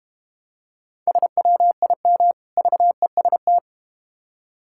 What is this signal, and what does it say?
Morse code, heard as a keyed tone, about 31 wpm, 700 Hz. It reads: SWIM VEST
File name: practice / 25wpm / Difficult Words - Speed-Racing with Hints - 25wpm